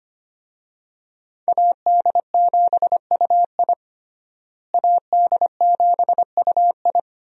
Morse code, heard as a keyed tone, about 25 wpm, 700 Hz. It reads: AD7UI AD7UI
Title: USA Call Signs - 5 Characters - Repeated 3x - 25wpm